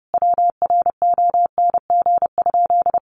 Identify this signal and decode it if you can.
Morse code, heard as a keyed tone, about 30 wpm, 700 Hz. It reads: WRONG?
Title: Sentences from Top 1100 Words 30wpm